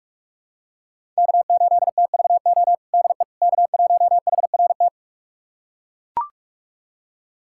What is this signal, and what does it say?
Morse code, heard as a keyed tone, about 41 wpm, 700 Hz. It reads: K8TVO DE K1HRT
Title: USA Callsign DE Callsign - 5 Characters - 45wpm